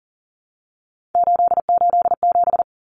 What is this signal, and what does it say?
Morse code, heard as a keyed tone, about 40 wpm, 700 Hz. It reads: 887
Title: Numbers - 3-Digit 40wpm